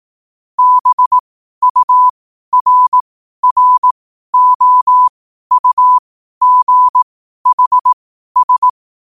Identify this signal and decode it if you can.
Morse code, heard as a keyed tone, about 18 wpm, spaced as for 12 wpm, 1000 Hz. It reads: BURROUGHS
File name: qotdfiles / 12wpm